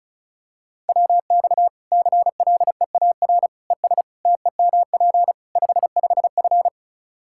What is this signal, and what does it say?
Morse code, heard as a keyed tone, about 36 wpm, 700 Hz. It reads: WX CLEAR ES TEMP 55F
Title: Practice QSO - Exchange 2 - Info - 35wpm